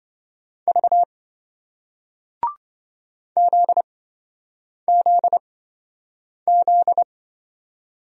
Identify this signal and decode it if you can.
Morse code, heard as a keyed tone, about 26 wpm, spaced as for 7 wpm, 700 Hz. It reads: V ZZZ